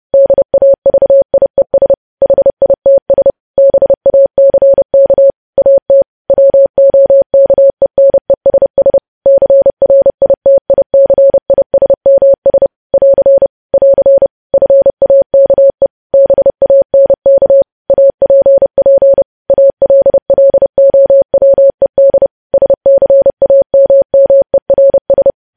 Morse code, heard as a keyed tone, about 30 wpm, 550 Hz. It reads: DAVIES HITS BACK AT WOKENESS CRITICISMS <AR> <AR> FAKE BANK APP ALLOWED SCAMMERS
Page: news headlines in morse code at 30 wpm